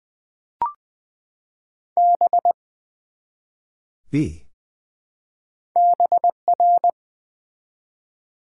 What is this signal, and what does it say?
Morse code, heard as a keyed tone, about 20 wpm, 700 Hz. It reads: B BR
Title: Build A Word - 1 Letter At A Time - 5-Letter Words - 20wpm